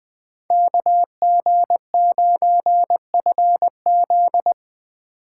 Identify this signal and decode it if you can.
Morse code, heard as a keyed tone, about 20 wpm, 700 Hz. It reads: KG9FZ
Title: Speed Racing - USA 2x2 Call Signs 17wpm